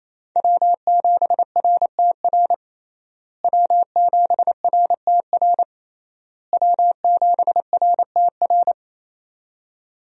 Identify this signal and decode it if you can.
Morse code, heard as a keyed tone, about 27 wpm, 700 Hz. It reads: W7RTR W7RTR W7RTR
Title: USA Call Signs - 5 Characters - Repeated 3x - 28wpm